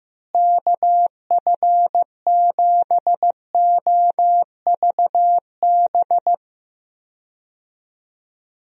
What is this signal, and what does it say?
Morse code, heard as a keyed tone, about 16 wpm, 700 Hz. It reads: KF7OVB
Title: USA Call Signs - 6 Characters - Repeated 2x - 15wpm